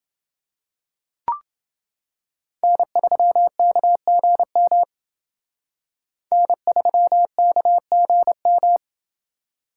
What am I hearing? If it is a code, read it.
Morse code, heard as a keyed tone, about 28 wpm, 700 Hz. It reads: N3KGM N3KGM